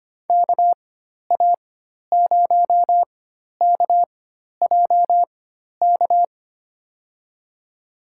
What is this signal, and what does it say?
Morse code, heard as a keyed tone, about 25 wpm, spaced as for 12 wpm, 700 Hz. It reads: KA0KJK
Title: USA Call Signs - 6 Characters - 4x Character Spacing - 25wpm